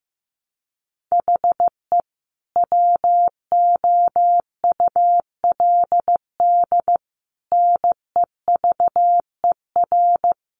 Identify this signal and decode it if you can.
Morse code, heard as a keyed tone, about 15 wpm, 700 Hz. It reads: HE WOULD NEVER
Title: Sentences from Top 1300 Words 15wpm